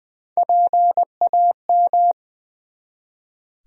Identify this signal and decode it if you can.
Morse code, heard as a keyed tone, about 20 wpm, 700 Hz. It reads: PAM
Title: USA Common Names - Max 3 Letters - 20wpm